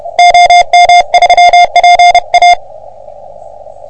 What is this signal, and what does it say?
Morse code, heard as a keyed tone, about 30 wpm, 720 Hz. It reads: OM3PA
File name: OM3PA PILUP CUC 388